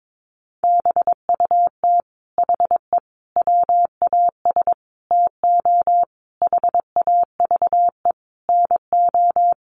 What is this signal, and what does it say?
Morse code, heard as a keyed tone, about 22 wpm, 700 Hz. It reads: BUT HE WAS TO HAVE NO
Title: Sentences from Top 1500 Words 22wpm